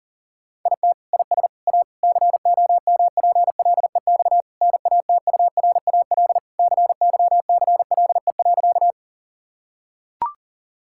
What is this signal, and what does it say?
Morse code, heard as a keyed tone, about 37 wpm, 700 Hz. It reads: IT IS A COMPLEX NATURAL CYCLE.
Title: Sentences from Top 1000 Words - Review Part 6 - 40wpm